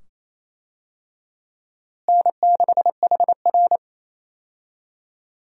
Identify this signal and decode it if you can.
Morse code, heard as a keyed tone, about 28 wpm, 700 Hz. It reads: N6HR
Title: USA Call Signs - 4 Characters - Repeated 2x - 28wpm